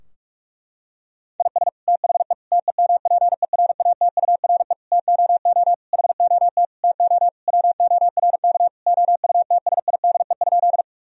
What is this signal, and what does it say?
Morse code, heard as a keyed tone, about 46 wpm, 700 Hz. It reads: IS THE TEMPERATURE TOO HOT TO WORK OUTSIDE?
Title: Sentences from Top 1000 Words - Review Part 4 - 45wpm